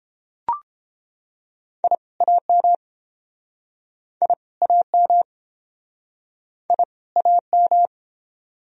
Morse code, heard as a keyed tone, about 31 wpm, 700 Hz. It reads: I AM I AM I AM